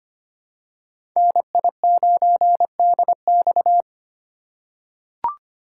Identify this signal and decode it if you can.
Morse code, heard as a keyed tone, about 25 wpm, 700 Hz. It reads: NI9DX
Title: USA Call Signs - 5 Characters - 3x Character Spacing - 25wpm